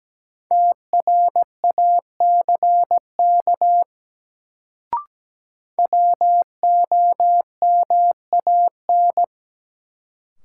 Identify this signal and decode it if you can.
Morse code, heard as a keyed tone, about 17 wpm, 700 Hz. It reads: TRACK WOMAN
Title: Sentences from Top 1100 Words - New Word Drill Alt - 17wpm